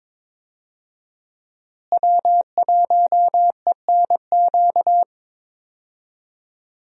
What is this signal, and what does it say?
Morse code, heard as a keyed tone, about 22 wpm, 700 Hz. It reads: W1ENQ
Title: USA Call Signs - 5 Characters - Repeated 3x - 22wpm